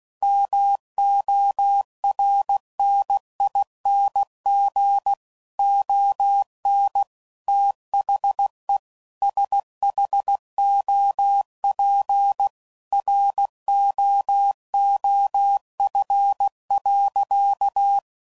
Morse code, heard as a keyed tone, about 16 wpm, 800 Hz. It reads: MORNING ON THE SHOP ROOF.
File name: CW Code Groups @15WPM / Weather Sets / Weather 4 28